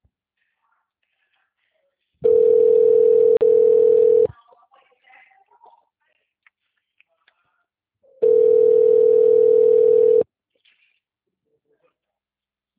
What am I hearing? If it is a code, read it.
Ringback tone (North America), 440 + 480 Hz.
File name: out- -2005- - - .80509